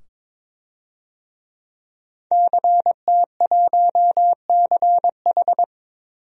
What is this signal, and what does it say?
Morse code, heard as a keyed tone, about 22 wpm, 700 Hz. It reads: CT1CH
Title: DXCC-100 Call Signs - 5 Characters - 22wpm